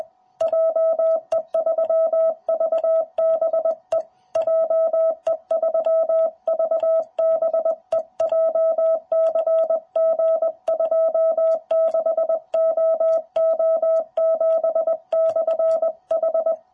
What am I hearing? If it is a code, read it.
Morse code, heard as a keyed tone, about 20 wpm, 660 Hz. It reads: JE3VBE JE3VBE JCG26OO7/H
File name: jh8cbh / 2021cw